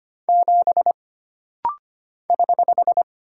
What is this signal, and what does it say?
Morse code, heard as a keyed tone, about 25 wpm, 700 Hz. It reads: 7 <HH>
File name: Single Character (Letter:Number:Punctuation:Prosign) - Rapid-Fire - 25wpm